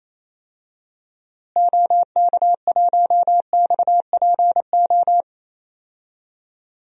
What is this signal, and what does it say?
Morse code, heard as a keyed tone, about 28 wpm, 700 Hz. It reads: OK1XPO
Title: DXCC-100 Call Signs - 6 Characters - Repeated 2x - 28wpm